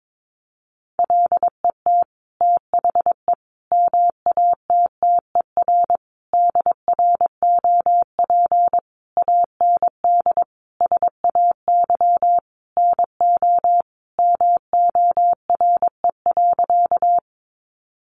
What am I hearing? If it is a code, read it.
Morse code, heard as a keyed tone, about 22 wpm, 700 Hz. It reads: LET THE MATTER DROP AND SAY NO MORE.
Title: Sentences from Top 1400 Words 22wpm